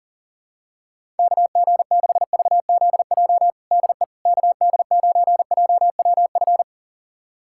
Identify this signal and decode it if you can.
Morse code, heard as a keyed tone, about 37 wpm, 700 Hz. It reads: KG6VZJ DE KD9JWF
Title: USA Callsign DE Callsign - 6 Characters - 40wpm